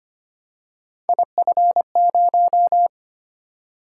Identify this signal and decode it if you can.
Morse code, heard as a keyed tone, about 25 wpm, 700 Hz. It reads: IF0
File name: Three Letter-Number Combinations - 25wpm